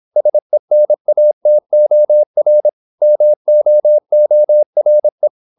Morse code, heard as a keyed tone, about 26 wpm, 600 Hz. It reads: SENATOR MOORE
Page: Never 26wpm | CW med Gnister